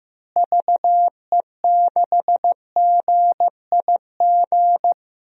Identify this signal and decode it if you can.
Morse code, heard as a keyed tone, about 15 wpm, 700 Hz. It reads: VE6GIG
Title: DXCC-100 Call Signs - 6 Characters - Repeated 3x - 15wpm